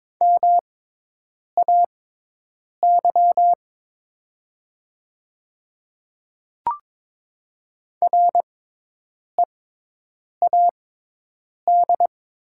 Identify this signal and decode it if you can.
Morse code, heard as a keyed tone, about 22 wpm, spaced as for 8 wpm, 690 Hz. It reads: MAY READ
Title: 6x Character Spacing - Top 300 Words - 22wpm